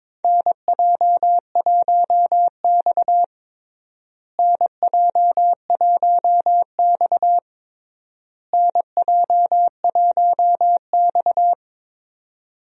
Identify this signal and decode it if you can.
Morse code, heard as a keyed tone, about 21 wpm, 700 Hz. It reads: NJ1X NJ1X NJ1X
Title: USA Call Signs - 4 Characters - Repeated 3x - 22wpm